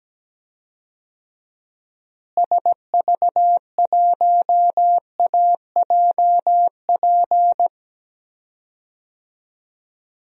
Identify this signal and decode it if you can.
Morse code, heard as a keyed tone, about 17 wpm, 700 Hz. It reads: SV1AJP
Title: DXCC-100 Call Signs - 6 Characters - Repeated 3x - 17wpm